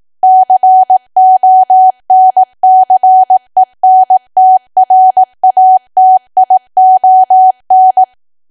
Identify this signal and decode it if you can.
Morse code, heard as a keyed tone, about 18 wpm, 750 Hz. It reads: CONCENTRATION
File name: cwa / W1AW-Level2 / 18WPM